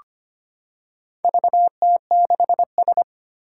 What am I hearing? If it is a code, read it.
Morse code, heard as a keyed tone, about 25 wpm, 700 Hz. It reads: VT6S
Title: Four Challenging Letter-Number Combinations 25wpm